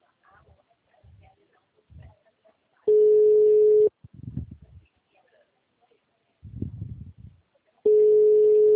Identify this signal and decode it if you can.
Ringback tone (Europe, 425 Hz).